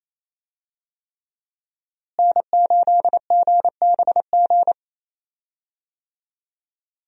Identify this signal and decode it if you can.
Morse code, heard as a keyed tone, about 28 wpm, 700 Hz. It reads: N8GBG
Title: USA Call Signs - 5 Characters - Repeated 3x - 28wpm